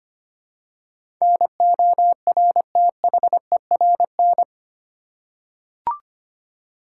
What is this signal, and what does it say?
Morse code, heard as a keyed tone, about 25 wpm, 690 Hz. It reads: NORTHERN
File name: 8 Letter Words - First 3 Letters Given - 25wpm